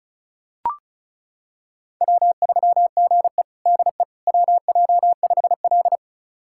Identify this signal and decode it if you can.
Morse code, heard as a keyed tone, about 36 wpm, 700 Hz. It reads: W3GE DE WJ5L
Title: USA Callsign DE Callsign - 4 Characters - 35wpm